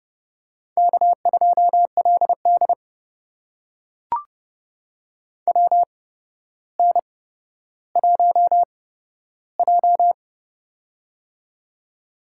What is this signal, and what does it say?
Morse code, heard as a keyed tone, about 29 wpm, spaced as for 13 wpm, 700 Hz. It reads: K2LD WN1J
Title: USA Call Signs - 8x Character Spacing - 30wpm